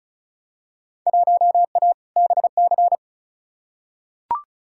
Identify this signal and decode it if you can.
Morse code, heard as a keyed tone, about 36 wpm, 700 Hz. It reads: 1A BC
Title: Field Day Exchanges - 35wpm